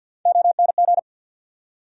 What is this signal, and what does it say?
Morse code, heard as a keyed tone, about 52 wpm, 700 Hz. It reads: ONG
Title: USA Call Sign Postfixes - 3 Letters 50wpm